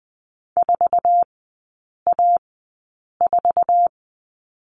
Morse code, heard as a keyed tone, about 20 wpm, spaced as for 8 wpm, 700 Hz. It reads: 4A4
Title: 047 Lesson 9e - Number 4 - Three Characters 20wpm